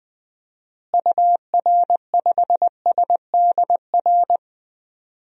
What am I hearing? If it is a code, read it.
Morse code, heard as a keyed tone, about 21 wpm, 700 Hz. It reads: UR5SDR
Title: DXCC-100 Call Signs - 6 Characters - Repeated 3x - 20wpm